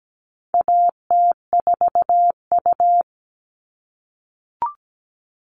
Morse code, heard as a keyed tone, about 17 wpm, 700 Hz. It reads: AT4U
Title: All Cut Numbers - 4-Digit 17wpm